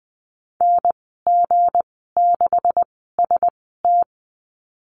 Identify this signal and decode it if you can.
Morse code, heard as a keyed tone, about 20 wpm, spaced as for 15 wpm, 700 Hz. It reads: NG6ST
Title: 167 Lesson 27g - Number 7 - Call Signs 20wpm